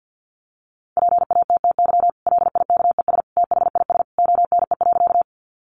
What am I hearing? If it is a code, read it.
Morse code, heard as a keyed tone, about 50 wpm, 700 Hz. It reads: PATTY LIKES THIS ONE.
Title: Additional Easy Phrases 50wpm